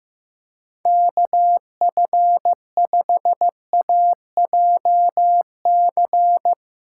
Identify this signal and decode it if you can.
Morse code, heard as a keyed tone, about 15 wpm, 700 Hz. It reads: KF5AJC
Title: USA Callsign DE Callsign - 6 Characters - 15wpm